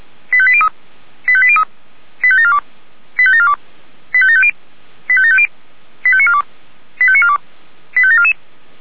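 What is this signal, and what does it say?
Five-tone selective call (ZVEI), heard as quick runs of five tones, tones of about 70 ms. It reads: 85803 85803 85742 85742 85750 85750 85942 85942 85744
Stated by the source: ZVEI five-tone selective call received via SDR.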